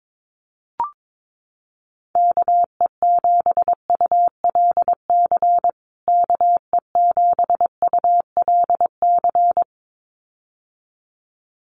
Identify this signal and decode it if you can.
Morse code, heard as a keyed tone, about 22 wpm, 700 Hz. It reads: KE7ULC KE7ULC
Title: USA Call Signs Repeated 2x 22wpm